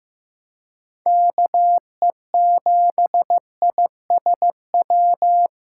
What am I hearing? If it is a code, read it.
Morse code, heard as a keyed tone, about 15 wpm, 700 Hz. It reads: KE7ISW
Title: USA Call Signs - 6 Characters - 2x Character Spacing - 15wpm